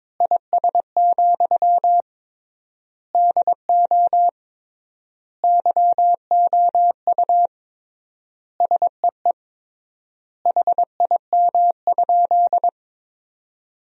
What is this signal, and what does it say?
Morse code, heard as a keyed tone, about 22 wpm, 700 Hz. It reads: IS, DO YOU SEE HIM?
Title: 3x Word-Spacing - Sentences from Top 100 Words - 22wpm